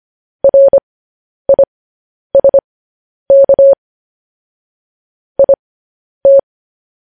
Morse code, heard as a keyed tone, about 25 wpm, spaced as for 10 wpm, 550 Hz. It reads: RISK IT
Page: News Headlines in Morse Code at 10 WPM